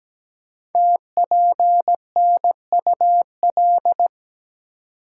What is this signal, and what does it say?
Morse code, heard as a keyed tone, about 17 wpm, 700 Hz. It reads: TPNUL
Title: Code Groups - 5 Letters - 17wpm